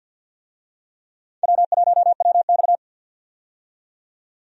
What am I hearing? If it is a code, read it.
Morse code, heard as a keyed tone, about 50 wpm, 700 Hz. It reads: W1WX
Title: USA Call Signs - 4 Characters - 50wpm